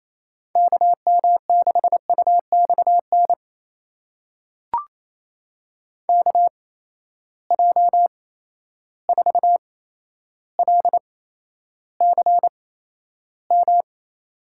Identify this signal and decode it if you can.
Morse code, heard as a keyed tone, about 28 wpm, spaced as for 12 wpm, 700 Hz. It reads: KM6UXN KJ4LCM
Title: USA Call Signs - 6 Characters - 8x Character Spacing - 28wpm